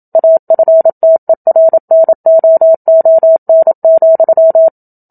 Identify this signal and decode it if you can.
Morse code, heard as a keyed tone, about 27 wpm, 650 Hz. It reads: AFTERNOON,
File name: Book Two / 01-Chapter 1 - 27 WPM